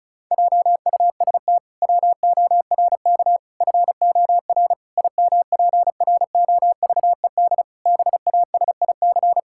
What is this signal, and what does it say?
Morse code, heard as a keyed tone, about 34 wpm, 700 Hz. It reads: JUST WORK FOR IMPROVED BASIC
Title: Pangram Challenge v2 - 35wpm